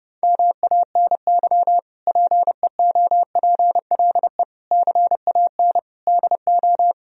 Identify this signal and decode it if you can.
Morse code, heard as a keyed tone, about 32 wpm, 700 Hz. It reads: MANY PEOPLE CAN DO
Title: Sentences from Top 1000 Words - Review Part 1 - 30wpm